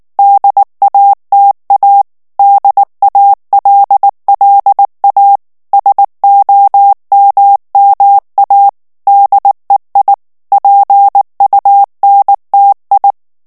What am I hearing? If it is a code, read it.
Morse code, heard as a keyed tone, about 19 wpm, 800 Hz. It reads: DATA DALLA SOMMA DEI PUNTI
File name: morse mp3 / ITALIEN texte clair 2 vitesse 18 mots minute